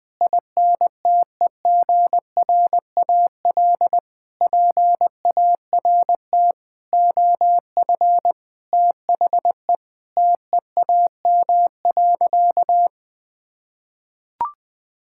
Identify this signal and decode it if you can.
Morse code, heard as a keyed tone, about 20 wpm, 700 Hz. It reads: INTEGRAL PART OF THE TEAM.
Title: Sentences with 5-Word Phrases - Spoken Beginning - 20wpm